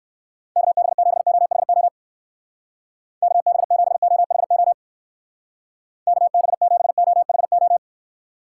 Morse code, heard as a keyed tone, about 55 wpm, 700 Hz. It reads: KB7OHO KB7OHO KB7OHO
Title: Speed Racing - USA 2x3 Call Signs 45wpm